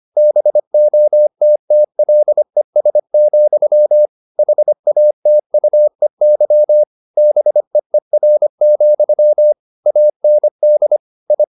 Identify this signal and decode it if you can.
Morse code, heard as a keyed tone, about 25 wpm, 600 Hz. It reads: BOTTLES, HATUEY BEER, AND I
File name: the old man and the sea part 1 25wpm